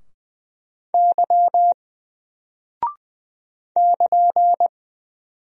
Morse code, heard as a keyed tone, about 20 wpm, 700 Hz. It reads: Y <KN>
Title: Single Character (Letter:Number:Punctuation:Prosign) 20wpm